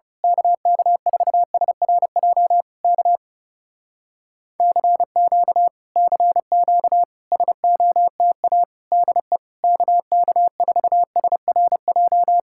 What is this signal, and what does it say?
Morse code, heard as a keyed tone, about 28 wpm, 700 Hz. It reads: KK4SRJ K CQ CQ SOTA DE KK4SRJ
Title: Speed-Racing Common QSO Phrases - 30wpm